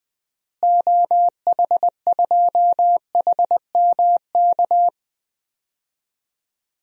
Morse code, heard as a keyed tone, about 20 wpm, 700 Hz. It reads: OH2HMK
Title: DXCC-100 Call Signs - 6 Characters - Repeated 3x - 20wpm